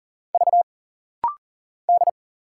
Morse code, heard as a keyed tone, about 39 wpm, 700 Hz. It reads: V D
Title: Single Letter-Number Alt - Rapid-Fire 40wpm